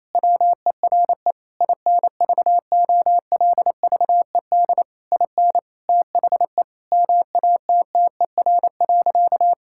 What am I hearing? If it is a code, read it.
Morse code, heard as a keyed tone, about 28 wpm, 700 Hz. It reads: WERE INVOLVED IN THE MATTER.
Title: Sentences from Top 1000 Words - Review Part 5 - 28wpm